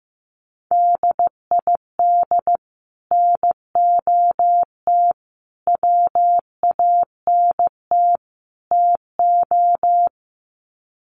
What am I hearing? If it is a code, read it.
Morse code, heard as a keyed tone, about 15 wpm, 700 Hz. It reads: DID NOT WANT TO
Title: Sets of 4 Words 15wpm